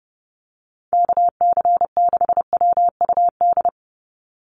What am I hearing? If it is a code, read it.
Morse code, heard as a keyed tone, about 30 wpm, 700 Hz. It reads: KC6WUD
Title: USA Call Signs 30wpm